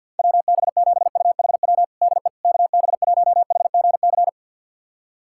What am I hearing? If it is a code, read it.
Morse code, heard as a keyed tone, about 51 wpm, 700 Hz. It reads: WB7UHW DE KB1HGC
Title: USA Callsign DE Callsign 50wpm